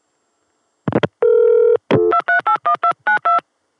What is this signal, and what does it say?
DTMF, heard as telephone touch tones, tones of about 100 ms. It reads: 2381292